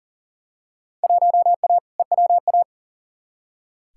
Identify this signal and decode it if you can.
Morse code, heard as a keyed tone, about 40 wpm, 700 Hz. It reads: 1A EWA
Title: Field Day Exchanges Encore - 40wpm